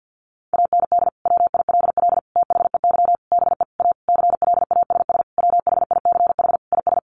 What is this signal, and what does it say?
Morse code, heard as a keyed tone, about 50 wpm, 700 Hz. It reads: AND WILL THEY BE A CLASS WHICH IS
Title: Sentences from Top 1400 Words 50wpm